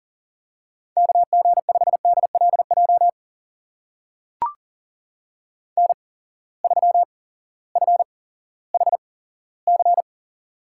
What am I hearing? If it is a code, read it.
Morse code, heard as a keyed tone, about 38 wpm, spaced as for 18 wpm, 700 Hz. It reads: KG5DLJ N3FHC
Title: USA Call Signs - 8x Character Spacing - 40wpm